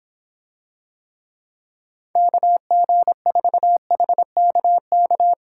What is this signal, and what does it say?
Morse code, heard as a keyed tone, about 24 wpm, 700 Hz. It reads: KG4HKK